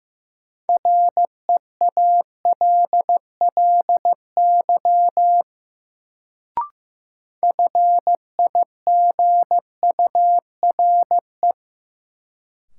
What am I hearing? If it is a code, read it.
Morse code, heard as a keyed tone, about 16 wpm, 700 Hz. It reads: REALLY FIGURE
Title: Sets of 4 Words - New Word Drill - 15wpm